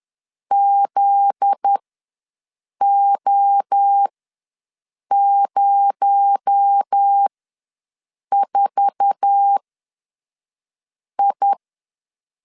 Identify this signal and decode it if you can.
Morse code, heard as a keyed tone, about 10 wpm, spaced as for 5 wpm, 800 Hz. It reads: ZO04I
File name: LEZIONI / 8 YJ45 / 8 YJ4 ws 10wpm 4020